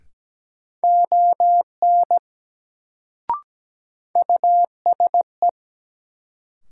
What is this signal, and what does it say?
Morse code, heard as a keyed tone, about 17 wpm, 700 Hz. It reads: ON USE
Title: Sets of 2 Words - New Word Drill - 17wpm